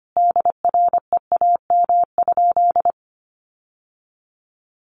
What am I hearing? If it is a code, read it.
Morse code, heard as a keyed tone, about 25 wpm, 700 Hz. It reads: DREAM?
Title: Sentences from Top 1500 Words 25wpm